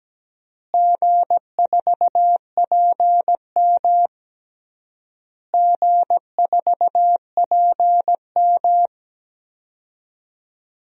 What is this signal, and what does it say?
Morse code, heard as a keyed tone, about 17 wpm, 700 Hz. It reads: G4PM G4PM
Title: DXCC-100 Call Signs - 4 Characters - Repeated 3x - 17wpm